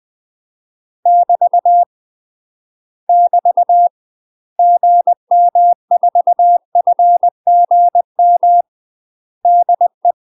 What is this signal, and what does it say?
Morse code, heard as a keyed tone, about 20 wpm, 700 Hz. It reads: <BT> <BT> GM4FGM DE
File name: UK MASTER QSOs TRAINING / 20WPM / UK MASTER QSOs 0013